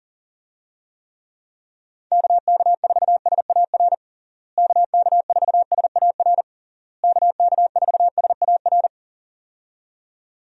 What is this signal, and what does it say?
Morse code, heard as a keyed tone, about 37 wpm, 700 Hz. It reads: KK4SAR KK4SAR KK4SAR
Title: USA Call Signs - 6 Characters - Repeated 3x - 40wpm